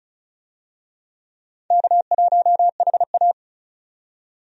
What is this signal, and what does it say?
Morse code, heard as a keyed tone, about 36 wpm, 700 Hz. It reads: K1HA
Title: USA Call Signs - 4 Characters - 35wpm